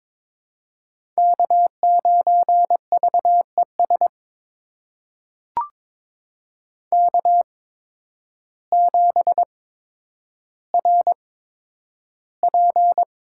Morse code, heard as a keyed tone, about 21 wpm, spaced as for 10 wpm, 700 Hz. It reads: K9VES K7RP
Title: USA Call Signs - 5 Characters - 8x Character Spacing - 22wpm